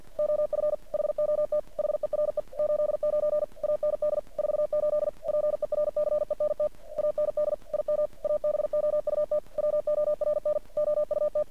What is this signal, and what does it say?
Morse code, heard as a keyed tone, about 50 wpm, 600 Hz. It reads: OF HOT HERE 89 AND 48 PERCENT AND IM ABOUT WORN OUT
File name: archived / CRSnet